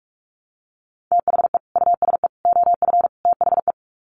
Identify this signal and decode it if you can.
Morse code, heard as a keyed tone, about 44 wpm, 700 Hz. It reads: THE USE OF THE
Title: Sets of 4 Words 45wpm